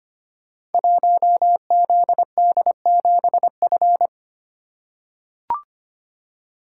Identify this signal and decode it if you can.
Morse code, heard as a keyed tone, about 25 wpm, 700 Hz. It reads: 1ZD7F
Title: Code Groups - 5 Letter:Numbers - 25wpm